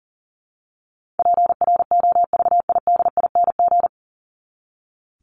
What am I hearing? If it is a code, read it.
Morse code, heard as a keyed tone, about 40 wpm, 700 Hz. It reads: PROVIDING
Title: ING Suffix - Top 100 Words with Base Words 40wpm